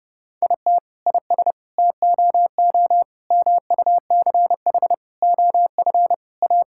Morse code, heard as a keyed tone, about 30 wpm, 700 Hz. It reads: IT IS TOO MUCH OF A
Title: Sentences from Top 1000 Words - Review Part 5 - 30wpm